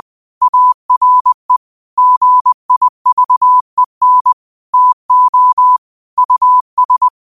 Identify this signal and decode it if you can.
Morse code, heard as a keyed tone, about 20 wpm, 1000 Hz. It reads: ARE GIVEN TO US
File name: qotdfiles / 20wpm